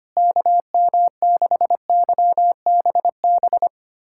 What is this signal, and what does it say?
Morse code, heard as a keyed tone, about 25 wpm, 700 Hz. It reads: KM6YBB
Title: USA Callsign DE Callsign - 6 Characters - 25wpm